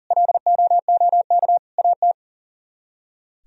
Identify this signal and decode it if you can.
Morse code, heard as a keyed tone, about 52 wpm, 700 Hz. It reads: LOOK AT